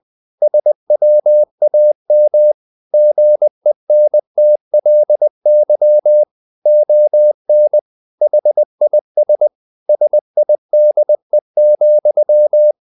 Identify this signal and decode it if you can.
Morse code, heard as a keyed tone, about 20 wpm, 600 Hz. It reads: SWAM GENTLY ON HIS SIDE,
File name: the old man and the sea part 2 20wpm